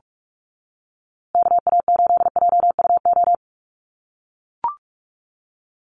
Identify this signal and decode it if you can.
Morse code, heard as a keyed tone, about 44 wpm, 700 Hz. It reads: KA8JUO
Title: USA Call Signs Repeated 2x 45wpm